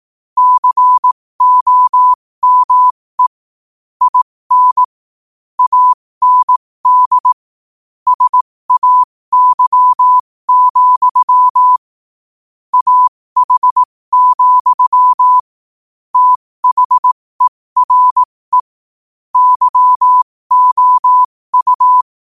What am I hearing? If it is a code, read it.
Morse code, heard as a keyed tone, about 18 wpm, spaced as for 15 wpm, 1000 Hz. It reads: COME IN AND SAY, AH, THERE YOU
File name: qotdfiles / 15wpm